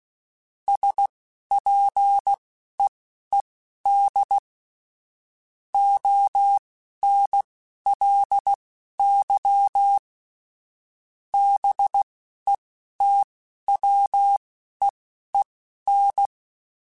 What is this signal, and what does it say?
Morse code, heard as a keyed tone, about 16 wpm, spaced as for 12 wpm, 800 Hz. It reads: SPEED ONLY BETWEEN